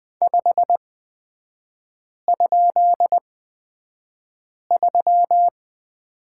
Morse code, heard as a keyed tone, about 20 wpm, spaced as for 5 wpm, 700 Hz. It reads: 5?3
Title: 255 Lesson 41e - Period - Three Characters 20wpm